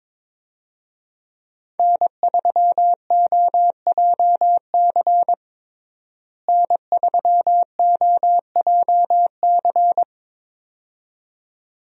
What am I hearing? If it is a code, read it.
Morse code, heard as a keyed tone, about 23 wpm, 700 Hz. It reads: N3OJC N3OJC
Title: USA Call Signs - 5 Characters - Repeated 2x - 22wpm